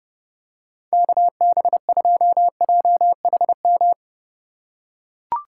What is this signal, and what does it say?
Morse code, heard as a keyed tone, about 29 wpm, 700 Hz. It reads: KB2JHM
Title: USA Call Signs - 6 Characters - 2x Character Spacing - 30wpm